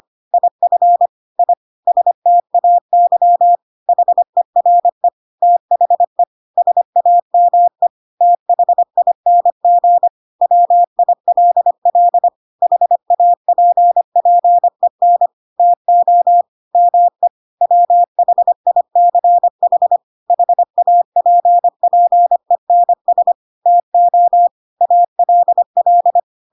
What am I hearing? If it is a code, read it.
Morse code, heard as a keyed tone, about 25 wpm, 700 Hz. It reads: IF I STAY HERE THE SAME THING WILL HAPPEN TO ME WHICH HAPPENS TO ALL
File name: Pinocchio / Pinocchio ch4 25 25 ik1hge